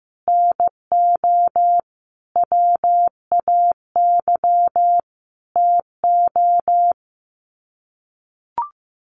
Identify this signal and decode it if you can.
Morse code, heard as a keyed tone, about 15 wpm, 700 Hz. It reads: NO WAY TO
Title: Sets of 5 Words Encore 15wpm